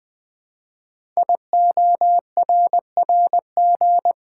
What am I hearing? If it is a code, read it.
Morse code, heard as a keyed tone, about 20 wpm, 700 Hz. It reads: IORRG
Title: Code Groups - 5 Letters - 20wpm